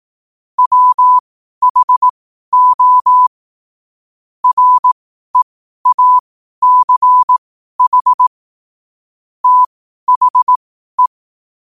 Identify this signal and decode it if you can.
Morse code, heard as a keyed tone, about 18 wpm, spaced as for 12 wpm, 1000 Hz. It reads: WHO REACH THE
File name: qotdfiles / 12wpm